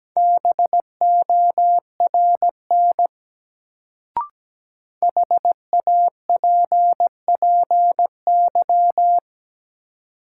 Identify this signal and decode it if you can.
Morse code, heard as a keyed tone, about 17 wpm, 700 Hz. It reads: BORN HAPPY
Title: Sentences from Top 1000 Words - New Word Drill - 17wpm